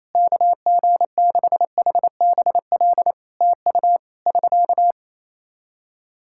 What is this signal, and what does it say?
Morse code, heard as a keyed tone, about 27 wpm, 700 Hz. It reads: KG6HBL TU <SK>
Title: Practice QSO - Exchange 4 - Ending - 28wpm